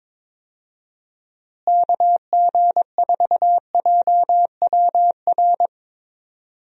Morse code, heard as a keyed tone, about 21 wpm, 700 Hz. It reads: KG4JWR
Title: USA Call Signs - 6 Characters - Repeated 2x - 22wpm